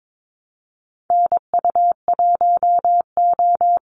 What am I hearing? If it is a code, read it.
Morse code, heard as a keyed tone, about 22 wpm, 700 Hz. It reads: NU1O
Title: Call Signs Repeated 2x - 22wpm